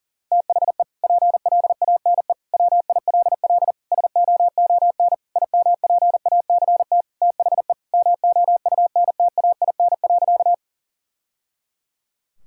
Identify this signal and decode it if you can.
Morse code, heard as a keyed tone, about 40 wpm, 700 Hz. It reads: THE PLANE WILL SOON IMPACT THE MOUNTAIN.
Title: Sentences from Top 1000 Words - Review Part 6 - 40wpm